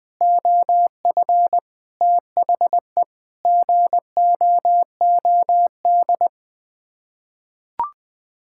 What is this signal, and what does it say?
Morse code, heard as a keyed tone, about 20 wpm, 700 Hz. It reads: OF THE GOOD
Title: Sets of 9 Words - 20wpm